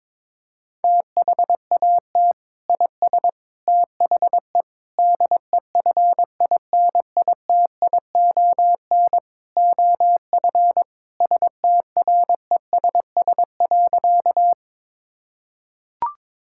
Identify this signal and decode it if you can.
Morse code, heard as a keyed tone, about 22 wpm, 700 Hz. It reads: THAT IS THE DEFINITION OF STRESS.
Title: Sentences from Top 1000 Words - Review Part 6 - 22wpm